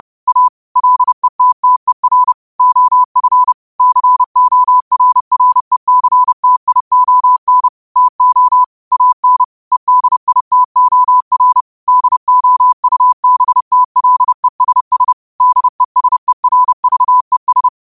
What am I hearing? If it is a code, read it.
Morse code, heard as a keyed tone, about 30 wpm, 1000 Hz. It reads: A LETTER OF CORRECTION TO AN EDITOR DOUBTLESS DESERVES
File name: qotdfiles / 30wpm